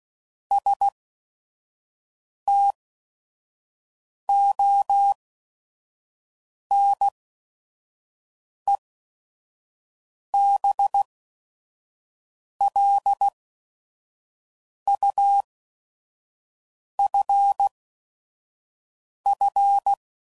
Morse code, heard as a keyed tone, about 16 wpm, spaced as for 5 wpm, 800 Hz. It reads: STONEBLUFF
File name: CW Code Groups @5WPM / fromSM's / Cities